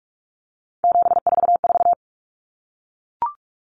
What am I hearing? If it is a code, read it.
Morse code, heard as a keyed tone, about 44 wpm, 700 Hz. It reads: 744
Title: Common Cut Numbers - 3-Digit 45wpm